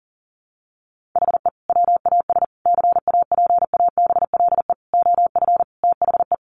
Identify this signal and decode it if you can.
Morse code, heard as a keyed tone, about 40 wpm, 700 Hz. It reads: HE WAS CAPABLE OF THE
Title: Sentences from Top 1200 Words 40wpm